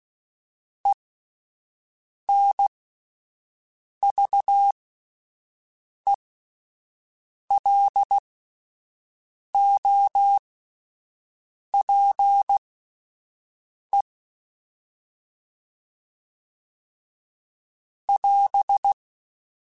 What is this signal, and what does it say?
Morse code, heard as a keyed tone, about 16 wpm, spaced as for 5 wpm, 800 Hz. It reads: ENVELOPE <AS>
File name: Alphabetical / Alphabetical mp3s / Es @5